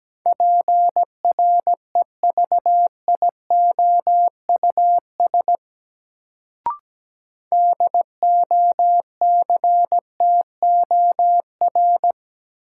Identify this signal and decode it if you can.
Morse code, heard as a keyed tone, about 17 wpm, 700 Hz. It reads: PREVIOUS DOCTOR
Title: Sentences from Top 1500 Words - New Word Drill - 17wpm